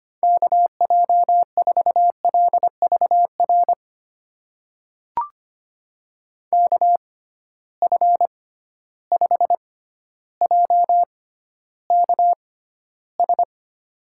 Morse code, heard as a keyed tone, about 25 wpm, spaced as for 13 wpm, 700 Hz. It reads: KJ4LVR KF5JKS
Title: USA Call Signs - 6 Characters - 6x Character Spacing - 25wpm